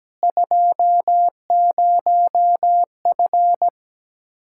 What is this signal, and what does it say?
Morse code, heard as a keyed tone, about 17 wpm, 700 Hz. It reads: 20F
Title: Speed-Racing Common QSO Phrases - 17wpm